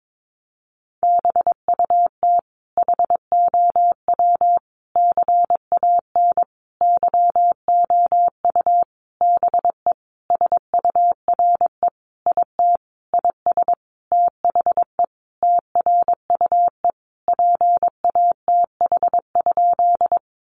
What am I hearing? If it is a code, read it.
Morse code, heard as a keyed tone, about 22 wpm, 700 Hz. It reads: BUT HOW CAN YOU BE SURE IT IS THE TRUE PATH?
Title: Sentences from Top 1400 Words 22wpm